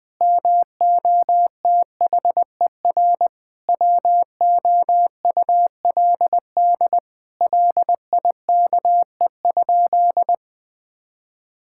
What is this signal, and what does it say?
Morse code, heard as a keyed tone, about 20 wpm, 700 Hz. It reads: MOTHER WOULD LIKE?
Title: Sentences from Top 1600 Words 20wpm